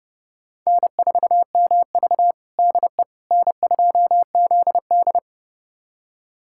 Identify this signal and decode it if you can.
Morse code, heard as a keyed tone, about 32 wpm, 700 Hz. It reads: N4MV DE N2ZD
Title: USA Callsign DE Callsign - 4 Characters - 30wpm